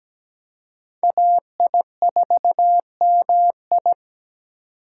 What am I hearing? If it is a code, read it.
Morse code, heard as a keyed tone, about 17 wpm, 700 Hz. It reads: AI4MI
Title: K1USN SST QSO Practice - 17wpm